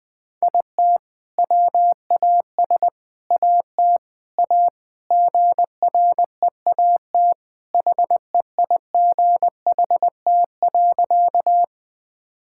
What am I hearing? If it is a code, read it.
Morse code, heard as a keyed tone, about 20 wpm, 700 Hz. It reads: IT WAS AT A GREAT HEIGHT.
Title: Sentences from Top 1600 Words 20wpm